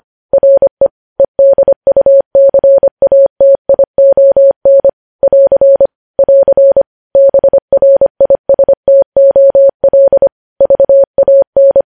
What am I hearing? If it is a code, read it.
Morse code, heard as a keyed tone, about 25 wpm, 550 Hz. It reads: RE EDUCATION <AR> <AR> BRISTOL VAN
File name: podcast / News headlines-1x-25x25